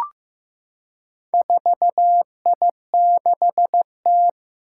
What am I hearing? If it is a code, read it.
Morse code, heard as a keyed tone, about 15 wpm, 700 Hz. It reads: 4I6T
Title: Four Challenging Letter-Number Combinations 15wpm